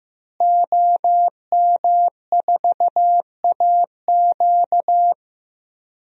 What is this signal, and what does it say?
Morse code, heard as a keyed tone, about 15 wpm, 700 Hz. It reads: OM4AQ
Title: DXCC-100 Call Signs - 5 Characters - Repeated 3x - 15wpm